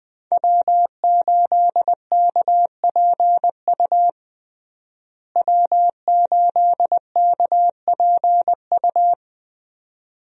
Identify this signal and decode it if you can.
Morse code, heard as a keyed tone, about 21 wpm, 700 Hz. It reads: W8KPU W8KPU
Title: USA Call Signs - 5 Characters - Repeated 2x - 20wpm